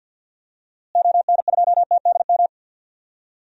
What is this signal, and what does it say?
Morse code, heard as a keyed tone, about 47 wpm, 700 Hz. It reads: ON2TDM
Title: DXCC-100 Call Signs - 6 Characters - Repeated 2x - 50wpm